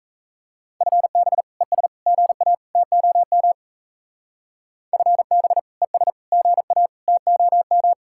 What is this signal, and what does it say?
Morse code, heard as a keyed tone, about 34 wpm, 700 Hz. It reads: FB ES GA TOM FB ES GA TOM
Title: Speed-Racing Common QSO Phrases Encore - 28wpm